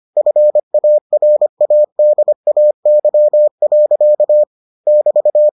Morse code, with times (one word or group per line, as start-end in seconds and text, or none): 0.2-4.4 FARADAY.
4.9-5.5 <BT>